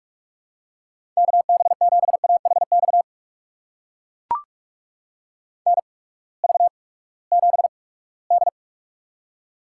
Morse code, heard as a keyed tone, about 44 wpm, spaced as for 22 wpm, 700 Hz. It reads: KB7AHX NV7D
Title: USA Call Signs - 8x Character Spacing - 45wpm